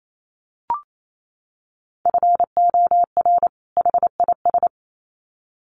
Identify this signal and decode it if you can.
Morse code, heard as a keyed tone, about 28 wpm, 700 Hz. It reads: FOR HIS
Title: Sets of 2 Words 28wpm